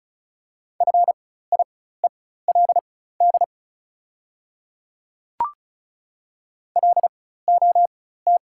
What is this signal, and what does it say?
Morse code, heard as a keyed tone, about 34 wpm, spaced as for 16 wpm, 690 Hz. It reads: FIELD LOT
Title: 4x Character Spacing - Top 500 Words - 35wpm